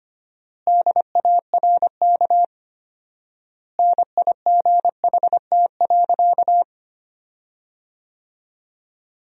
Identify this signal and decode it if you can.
Morse code, heard as a keyed tone, about 25 wpm, 700 Hz. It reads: DARK NIGHT.
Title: 4x Word-Spacing - Sentences from Top 500 Words - 25wpm